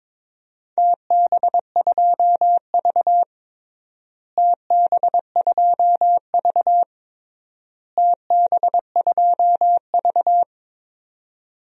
Morse code, heard as a keyed tone, about 21 wpm, 700 Hz. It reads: TB2V TB2V TB2V
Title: DXCC-100 Call Signs - 4 Characters - Repeated 3x - 22wpm